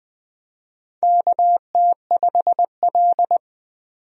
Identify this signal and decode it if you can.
Morse code, heard as a keyed tone, about 20 wpm, 700 Hz. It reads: KT5L
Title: USA Call Signs - 4 Characters - 20wpm